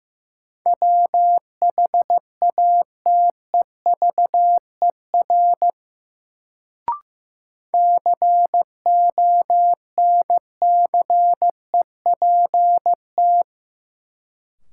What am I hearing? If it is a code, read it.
Morse code, heard as a keyed tone, about 15 wpm, 700 Hz. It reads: WHATEVER CONCEPT
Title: Sentences from Top 1100 Words - New Word Drill - 15wpm